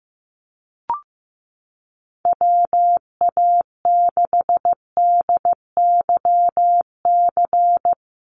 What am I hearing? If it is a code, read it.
Morse code, heard as a keyed tone, about 15 wpm, 700 Hz. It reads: WA6DYC
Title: USA Call Signs Repeated 3x 15wpm